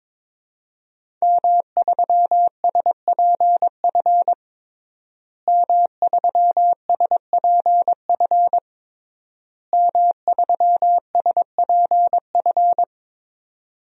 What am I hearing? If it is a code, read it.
Morse code, heard as a keyed tone, about 22 wpm, 700 Hz. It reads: M3SPF M3SPF M3SPF
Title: DXCC-100 Call Signs - 5 Characters - Repeated 3x - 22wpm